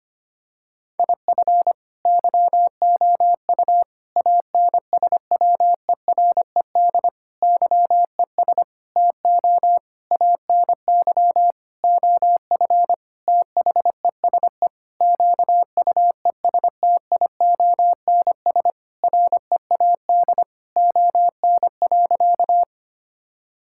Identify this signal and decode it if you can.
Morse code, heard as a keyed tone, about 25 wpm, 700 Hz. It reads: IF YOU ANSWERED YES TO ANY OF THESE QUESTIONS READ ON.
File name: practice / 25wpm / Sentences with 9-Word Phrases - 25wpm